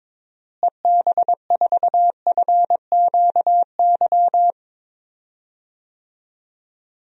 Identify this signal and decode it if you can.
Morse code, heard as a keyed tone, about 22 wpm, 700 Hz. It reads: EB4FQY
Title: DXCC-100 Call Signs - 6 Characters - Repeated 3x - 22wpm